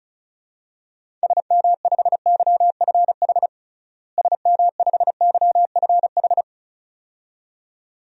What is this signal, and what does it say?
Morse code, heard as a keyed tone, about 34 wpm, 700 Hz. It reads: SM5YFH SM5YFH
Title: DXCC-100 Call Signs - 6 Characters - Repeated 2x - 35wpm